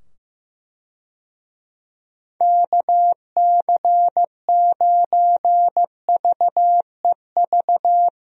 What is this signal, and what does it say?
Morse code, heard as a keyed tone, about 15 wpm, 700 Hz. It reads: KC9VEV
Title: USA Call Signs - 6 Characters - 15wpm